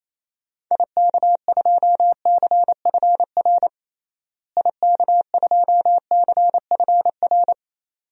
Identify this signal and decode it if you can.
Morse code, heard as a keyed tone, about 28 wpm, 700 Hz. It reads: IK2CFR IK2CFR
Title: DXCC-100 Call Signs - 6 Characters - Repeated 2x - 28wpm